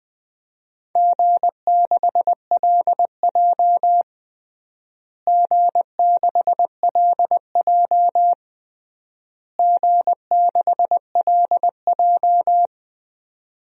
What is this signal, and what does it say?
Morse code, heard as a keyed tone, about 20 wpm, 700 Hz. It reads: G6LJ G6LJ G6LJ
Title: DXCC-100 Call Signs - 4 Characters - Repeated 3x - 20wpm